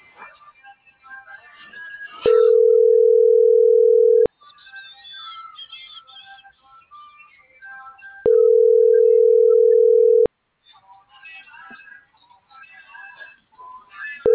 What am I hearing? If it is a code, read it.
Ringback tone (North America), 440 + 480 Hz.
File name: CENTRAL BLN / out ericka.canales